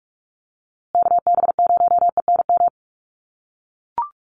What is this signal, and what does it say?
Morse code, heard as a keyed tone, about 44 wpm, 700 Hz. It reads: KB0ENM
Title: USA Call Signs Repeated 2x 45wpm